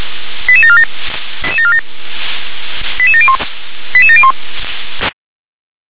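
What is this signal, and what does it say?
Five-tone selective call (ZVEI), heard as quick runs of five tones, tones of about 70 ms. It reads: 88747 88701 88701
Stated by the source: ZVEI five-tone call received on SDR.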